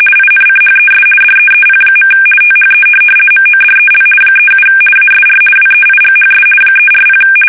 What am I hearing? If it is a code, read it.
RTTY, about 50 baud. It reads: INCHES DEEP. THE MODEL 37 ASR WEIGHS 340
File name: fsk850-50B / teletype-corp